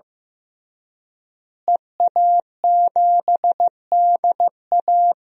Morse code, heard as a keyed tone, about 15 wpm, 700 Hz. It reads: EA7DA
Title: DXCC-100 Call Signs - 5 Characters - Repeated 3x - 15wpm